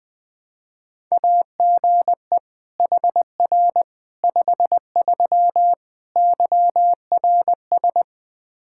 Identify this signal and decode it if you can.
Morse code, heard as a keyed tone, about 20 wpm, 700 Hz. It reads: AGE HR 53 YRS
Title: Practice QSO - Exchange 3 - More Info - 20wpm